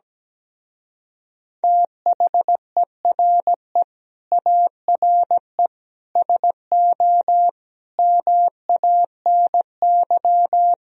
Morse code, heard as a keyed tone, about 17 wpm, 700 Hz. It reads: THERE ARE SO MANY
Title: Speed-Racing - Sets of 4 Words 17wpm